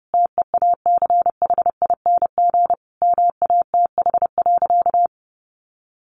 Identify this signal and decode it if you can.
Morse code, heard as a keyed tone, about 30 wpm, 700 Hz. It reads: TEACHING MATH.
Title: Easy Phrases 30wpm